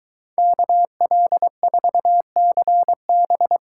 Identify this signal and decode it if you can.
Morse code, heard as a keyed tone, about 23 wpm, 700 Hz. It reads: KL4CB